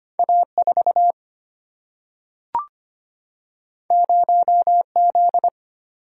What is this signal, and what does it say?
Morse code, heard as a keyed tone, about 25 wpm, 700 Hz. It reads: A4 0Z
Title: Single-Letter Single-Number 25wpm